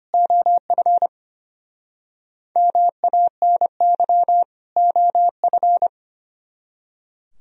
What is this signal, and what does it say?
Morse code, heard as a keyed tone, about 32 wpm, 700 Hz. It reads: OF MANY OF